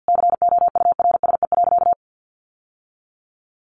Morse code, heard as a keyed tone, about 50 wpm, 700 Hz. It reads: COURSE.
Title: Sentences from Top 1100 Words 50wpm